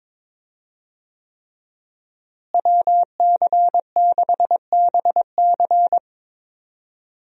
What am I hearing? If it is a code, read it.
Morse code, heard as a keyed tone, about 22 wpm, 700 Hz. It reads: WC6BC
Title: USA Call Signs - 5 Characters - Repeated 3x - 22wpm